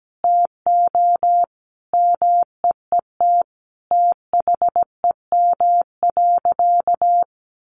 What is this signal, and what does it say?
Morse code, heard as a keyed tone, about 17 wpm, 700 Hz. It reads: TO MEET THEM.
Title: Sentences from Top 1100 Words 17wpm